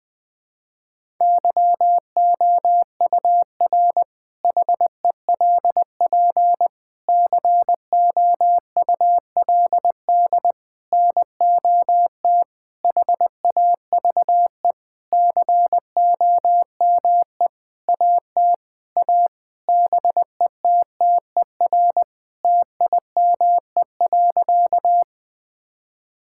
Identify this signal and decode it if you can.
Morse code, heard as a keyed tone, about 20 wpm, 700 Hz. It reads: YOUR HELP COULD NOT HAVE COME AT A BETTER TIME.
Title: Sentences with 8-Word Phrases - 20wpm